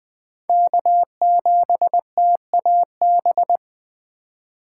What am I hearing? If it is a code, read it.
Morse code, heard as a keyed tone, about 20 wpm, 700 Hz. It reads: K7TAB
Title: 2023 SOTA Chaser Call Signs - 20wpm